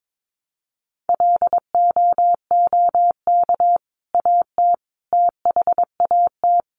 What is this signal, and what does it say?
Morse code, heard as a keyed tone, about 22 wpm, 700 Hz. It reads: LOOK AT THAT
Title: Sentences from Top 1100 Words 22wpm